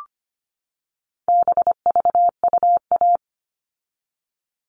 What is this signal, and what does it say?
Morse code, heard as a keyed tone, about 25 wpm, 700 Hz. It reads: BVUA
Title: All Cut Numbers - 4-Digit 25wpm